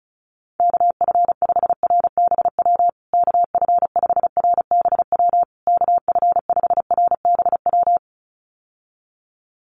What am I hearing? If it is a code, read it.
Morse code, heard as a keyed tone, about 34 wpm, 700 Hz. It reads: KF5RBW KF5RBW KF5RBW
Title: USA Call Signs Repeated 3x 35wpm